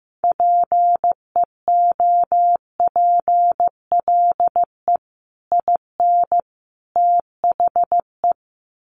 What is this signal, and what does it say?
Morse code, heard as a keyed tone, about 15 wpm, 700 Hz. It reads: PEOPLE IN THE
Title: Sets of 3 Words 15wpm